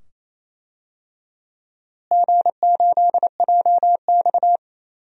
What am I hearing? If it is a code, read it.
Morse code, heard as a keyed tone, about 27 wpm, 700 Hz. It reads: G8JX
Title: DXCC-100 Call Signs - 4 Characters - Repeated 2x - 28wpm